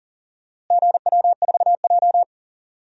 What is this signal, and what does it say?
Morse code, heard as a keyed tone, about 40 wpm, 700 Hz. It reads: GW4J
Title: DXCC-100 Call Signs - 4 Characters - Repeated 3x - 40wpm